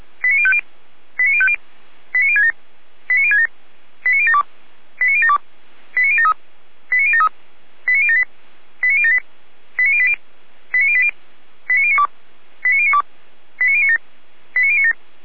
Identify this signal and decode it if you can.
Five-tone selective call (ZVEI), heard as quick runs of five tones, tones of about 70 ms. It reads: 89050 89050 89067 89067 89072 89072 89073 89073 89078 89078 89080 89080 89092 89092 89097 89097
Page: SDR/ZVEI